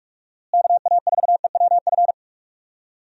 Morse code, heard as a keyed tone, about 46 wpm, 700 Hz. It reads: KA4EWF
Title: USA Call Signs - 6 Characters - Repeated 3x - 45wpm